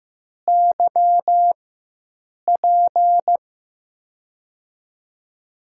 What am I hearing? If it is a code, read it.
Morse code, heard as a keyed tone, about 15 wpm, spaced as for 7 wpm, 700 Hz. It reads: YP
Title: USA Call Signs - 6 Characters - 4x Character Spacing - 15wpm